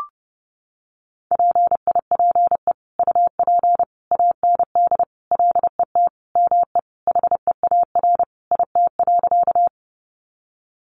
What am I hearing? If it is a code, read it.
Morse code, heard as a keyed tone, about 30 wpm, 700 Hz. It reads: PIPE UP AND LET ME HEAR IT.
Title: Sentences from Top 1500 Words 30wpm